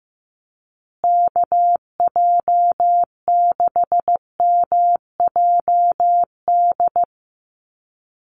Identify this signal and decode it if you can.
Morse code, heard as a keyed tone, about 15 wpm, 700 Hz. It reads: KJ6MJD
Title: USA Call Signs Repeated 3x 15wpm